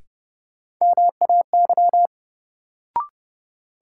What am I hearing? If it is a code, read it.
Morse code, heard as a keyed tone, about 32 wpm, 700 Hz. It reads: MAY
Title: Sentences from Top 100 Words - New Word Drill - 30wpm